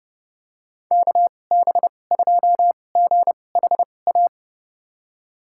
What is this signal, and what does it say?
Morse code, heard as a keyed tone, about 30 wpm, spaced as for 22 wpm, 700 Hz. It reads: KB2GHA
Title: USA Call Signs - 6 Characters - 2x Character Spacing - 30wpm